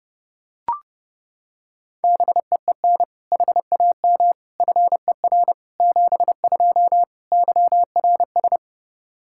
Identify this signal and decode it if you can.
Morse code, heard as a keyed tone, about 30 wpm, 700 Hz. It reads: BEEN HAM FER 72 YRS
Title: Practice QSO - Exchange 3 - More Info - 30wpm